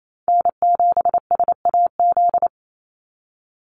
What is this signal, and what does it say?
Morse code, heard as a keyed tone, about 28 wpm, 700 Hz. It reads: N7SAZ
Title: USA Call Signs 28wpm